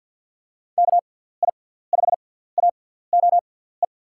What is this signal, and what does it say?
Morse code, heard as a keyed tone, about 52 wpm, spaced as for 17 wpm, 700 Hz. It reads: KI5AOE
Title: USA Call Signs - 6 Characters - 6x Character Spacing - 50wpm